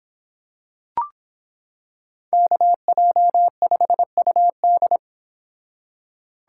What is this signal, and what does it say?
Morse code, heard as a keyed tone, about 26 wpm, 700 Hz. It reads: KJ5UD